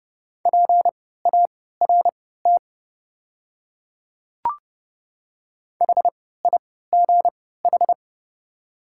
Morse code, heard as a keyed tone, about 32 wpm, spaced as for 17 wpm, 700 Hz. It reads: PART HIGH
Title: 3x Character Spacing - Top 300 Words - 30wpm